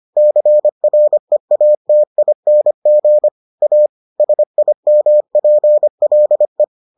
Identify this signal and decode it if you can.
Morse code, heard as a keyed tone, about 25 wpm, 600 Hz. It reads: CREATING A SIMPLE